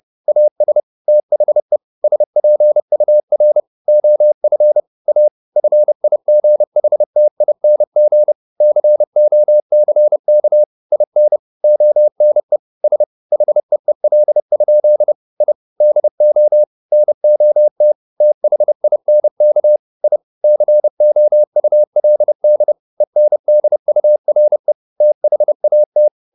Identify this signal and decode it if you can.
Morse code, heard as a keyed tone, about 30 wpm, 600 Hz. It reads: AS THE SPUR OF A FIGHTING COCK IN ONE S HEEL? I DO NOT THINK I COULD ENDURE THAT
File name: the old man and the sea / the old man and the sea 1 30wpm